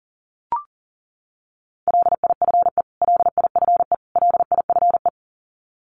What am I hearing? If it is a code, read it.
Morse code, heard as a keyed tone, about 40 wpm, 700 Hz. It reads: LIFE LIFE LIFE
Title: Top 500 Words Repeated 40wpm